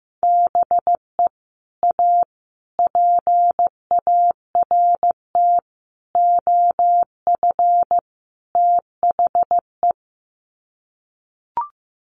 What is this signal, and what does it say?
Morse code, heard as a keyed tone, about 14 wpm, 700 Hz. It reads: BE A PART OF THE
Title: Sets of 5 Words Encore 15wpm